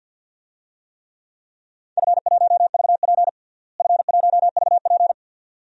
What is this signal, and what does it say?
Morse code, heard as a keyed tone, about 52 wpm, 700 Hz. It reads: F1VP F1VP
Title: DXCC-100 Call Signs - 4 Characters - Repeated 2x - 50wpm